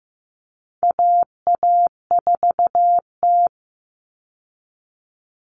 Morse code, heard as a keyed tone, about 15 wpm, 700 Hz. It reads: AA4T
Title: All Cut Numbers - 4-Digit 15wpm